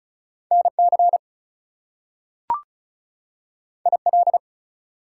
Morse code, heard as a keyed tone, about 36 wpm, 700 Hz. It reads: NC IL
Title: US States and Canadian Provinces 35wpm